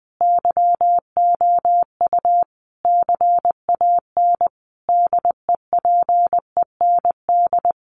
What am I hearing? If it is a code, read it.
Morse code, heard as a keyed tone, about 20 wpm, 700 Hz. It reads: YOU CAN DEPEND
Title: Sentences from Top 1300 Words 20wpm